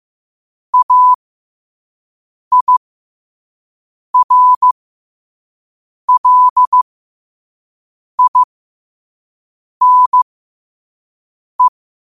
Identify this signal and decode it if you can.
Morse code, heard as a keyed tone, about 15 wpm, spaced as for 5 wpm, 1000 Hz. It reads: AIRLINE
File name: qotdfiles / 5wpm